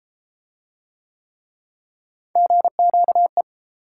Morse code, heard as a keyed tone, about 30 wpm, 700 Hz. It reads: GQE